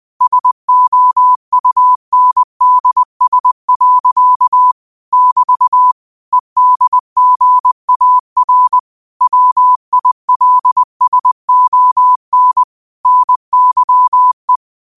Morse code, heard as a keyed tone, about 20 wpm, 1000 Hz. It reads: SOUNDS. <BT> EDGAR WILSON NYE
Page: Quotes for Sat, 16 Aug 2025 in Morse Code at 20 words per minute.